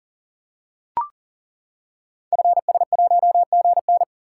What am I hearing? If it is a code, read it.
Morse code, heard as a keyed tone, about 40 wpm, 690 Hz. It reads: FS1GN
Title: Code Groups - 5 Letter:Numbers - 40wpm